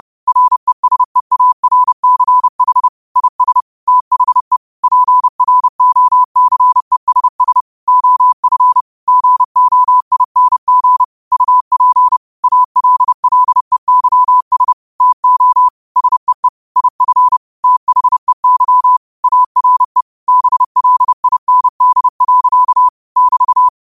Morse code, heard as a keyed tone, about 30 wpm, 1000 Hz. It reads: RESEARCH IS THE PROCESS OF GOING UP ALLEYS TO SEE IF THEY ARE BLIND. <BT>